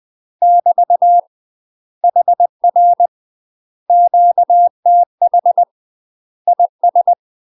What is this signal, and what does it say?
Morse code, heard as a keyed tone, about 20 wpm, 700 Hz. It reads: <BT> HR QTH IS
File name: UK MASTER QSOs TRAINING / 20WPM / UK MASTER QSOs 0011